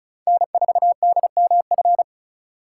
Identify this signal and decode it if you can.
Morse code, heard as a keyed tone, about 36 wpm, 700 Hz. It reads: N4DMF
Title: USA Call Signs - 5 Characters - 3x Character Spacing - 35wpm